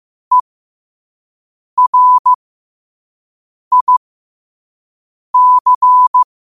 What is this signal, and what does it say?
Morse code, heard as a keyed tone, about 15 wpm, spaced as for 5 wpm, 1000 Hz. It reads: ERIC
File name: 5wpm